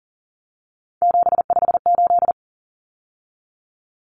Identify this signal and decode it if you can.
Morse code, heard as a keyed tone, about 40 wpm, 700 Hz. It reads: 758
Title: Numbers - 3-Digit 40wpm